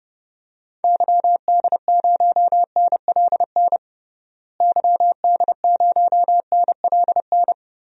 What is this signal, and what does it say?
Morse code, heard as a keyed tone, about 30 wpm, 700 Hz. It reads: YD0NLN YD0NLN
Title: DXCC-100 Call Signs - 6 Characters - Repeated 3x - 30wpm